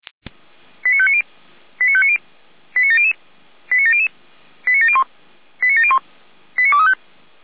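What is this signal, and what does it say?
Five-tone selective call (ZVEI), heard as quick runs of five tones, tones of about 70 ms. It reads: 89590 89590 89700 89700 89701 89701 89235
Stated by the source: ZVEI five-tone selective call received on SDR.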